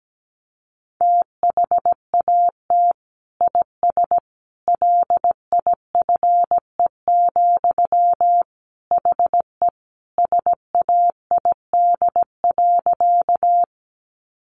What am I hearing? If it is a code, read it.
Morse code, heard as a keyed tone, about 17 wpm, 700 Hz. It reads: THAT IS LIFE, HE SAID.
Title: Sentences from Top 900 Words 17wpm